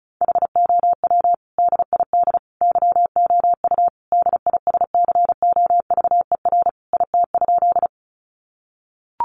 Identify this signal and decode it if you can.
Morse code, heard as a keyed tone, about 34 wpm, 700 Hz. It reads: HOW DID YOU DISCOVER IT?
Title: Sentences from Top 1100 Words 35wpm